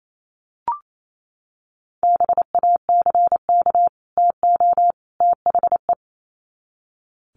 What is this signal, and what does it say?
Morse code, heard as a keyed tone, about 28 wpm, 700 Hz. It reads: BACK TO THE
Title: Sets of 3 Words 28wpm